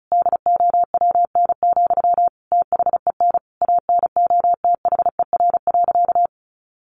Morse code, heard as a keyed tone, about 34 wpm, 700 Hz. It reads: DOWN, THEN ANOTHER.
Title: Sentences from Top 1400 Words 35wpm